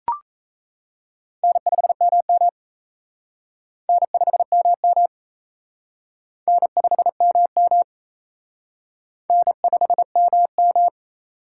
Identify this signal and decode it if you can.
Morse code, heard as a keyed tone, about 34 wpm, 700 Hz. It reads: N5MM N5MM N5MM N5MM
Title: Speed Racing - USA 1x2 Call Signs 28wpm